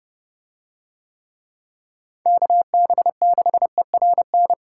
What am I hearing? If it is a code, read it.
Morse code, heard as a keyed tone, about 30 wpm, 700 Hz. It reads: KB6ERN